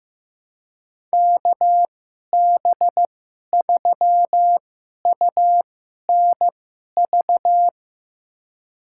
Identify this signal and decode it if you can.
Morse code, heard as a keyed tone, about 15 wpm, spaced as for 11 wpm, 700 Hz. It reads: KB3UNV
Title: USA Call Signs - 2x Character Spacing - 15wpm